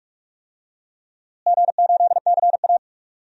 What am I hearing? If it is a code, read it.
Morse code, heard as a keyed tone, about 46 wpm, 700 Hz. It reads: G8CA
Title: DXCC-100 Call Signs - 4 Characters - Repeated 2x - 45wpm